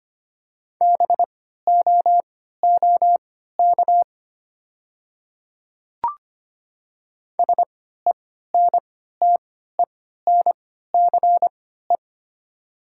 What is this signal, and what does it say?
Morse code, heard as a keyed tone, about 25 wpm, spaced as for 14 wpm, 700 Hz. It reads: BOOK SENTENCE
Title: 3x Character Spacing - Top 300 Words - 25wpm